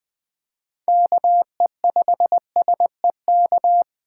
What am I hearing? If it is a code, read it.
Morse code, heard as a keyed tone, about 20 wpm, 700 Hz. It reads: KE5SEK